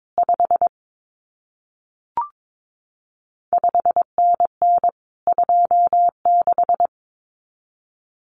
Morse code, heard as a keyed tone, about 22 wpm, 700 Hz. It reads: 5 5NN 26
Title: CQ WW DX Contest Exchange Encore - 22wpm